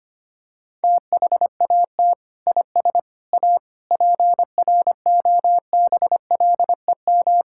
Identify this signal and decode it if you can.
Morse code, heard as a keyed tone, about 25 wpm, 700 Hz. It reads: THAT IS A PROBLEM
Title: Sentences from Top 1600 Words 25wpm